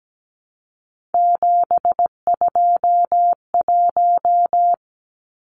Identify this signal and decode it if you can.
Morse code, heard as a keyed tone, about 17 wpm, 700 Hz. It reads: 721
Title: Numbers - 3-Digit 17wpm